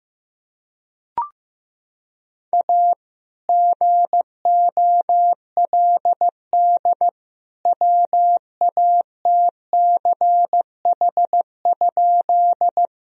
Morse code, heard as a keyed tone, about 15 wpm, 700 Hz. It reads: A GOLD WATCH?
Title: Sentences from Top 1000 Words - Review Part 3 - 15wpm